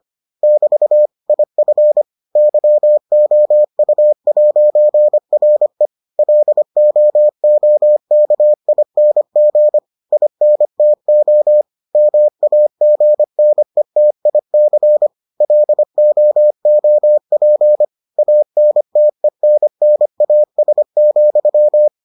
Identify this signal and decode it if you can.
Morse code, heard as a keyed tone, about 25 wpm, 600 Hz. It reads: <BT> IF YOU'RE LOOKING INTO MAGNETIC LOOP ANTENNAS,